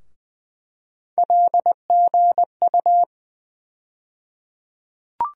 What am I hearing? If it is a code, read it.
Morse code, heard as a keyed tone, about 20 wpm, 700 Hz. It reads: LGU
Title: USA Call Sign Postfixes - 3 Letters 20wpm